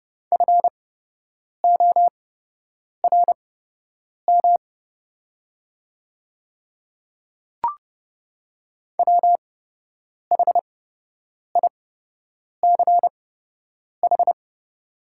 Morse code, heard as a keyed tone, about 30 wpm, spaced as for 8 wpm, 700 Hz. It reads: FORM WHICH
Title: 8x Character Spacing - Top 200 Words - 30wpm